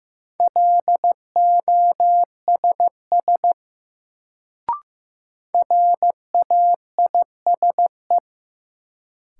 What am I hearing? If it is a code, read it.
Morse code, heard as a keyed tone, about 15 wpm, 700 Hz. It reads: LOSS RAISE
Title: Sentences from Top 1400 Words - New Word Drill Alt - 15wpm